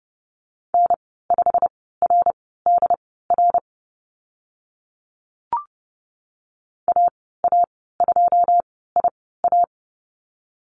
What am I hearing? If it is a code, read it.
Morse code, heard as a keyed tone, about 30 wpm, spaced as for 17 wpm, 700 Hz. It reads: N5RDR AA2IA
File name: Lesson 15g - Number 5 - Call Signs 30wpm